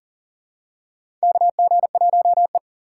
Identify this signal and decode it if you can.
Morse code, heard as a keyed tone, about 43 wpm, 690 Hz. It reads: KG1E
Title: USA Call Signs - 4 Characters - 2x Character Spacing - 40wpm